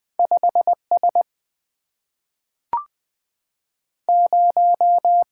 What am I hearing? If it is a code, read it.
Morse code, heard as a keyed tone, about 20 wpm, 690 Hz. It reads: 5 S 0
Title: Two Letter-Number Combinations - 20wpm